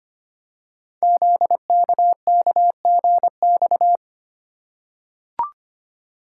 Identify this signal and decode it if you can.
Morse code, heard as a keyed tone, about 25 wpm, 700 Hz. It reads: ZKKGX
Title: Code Groups - 5 Letter:Numbers - 25wpm